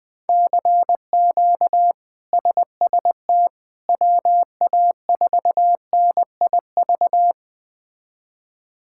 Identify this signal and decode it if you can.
Morse code, heard as a keyed tone, about 20 wpm, 700 Hz. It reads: CQ SST WA4NIV
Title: K1USN SST QSO Practice - 20wpm